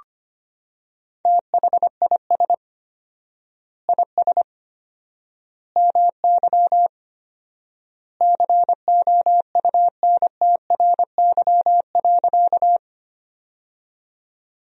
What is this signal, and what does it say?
Morse code, heard as a keyed tone, about 25 wpm, 700 Hz. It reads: THIS IS MY COUNTRY.
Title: 4x Word-Spacing - Sentences from Top 300 Words - 25wpm